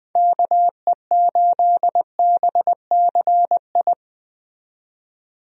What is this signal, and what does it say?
Morse code, heard as a keyed tone, about 20 wpm, 700 Hz. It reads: KE8BCI
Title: USA Call Signs - 6 Characters - 20wpm